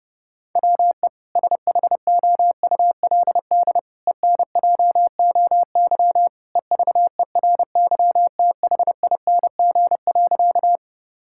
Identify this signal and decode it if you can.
Morse code, heard as a keyed tone, about 30 wpm, 700 Hz. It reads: WE SHOULD ENJOY EVERYTHING.
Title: Sentences from Top 1000 Words - Review Part 5 - 30wpm